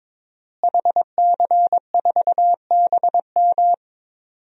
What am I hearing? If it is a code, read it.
Morse code, heard as a keyed tone, about 22 wpm, 700 Hz. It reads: HC4BM
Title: DXCC-100 Call Signs - 5 Characters - Repeated 3x - 22wpm